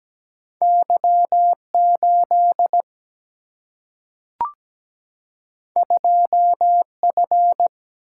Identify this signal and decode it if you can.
Morse code, heard as a keyed tone, about 17 wpm, 700 Hz. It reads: Y8 2F
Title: Single-Letter Single-Number 17wpm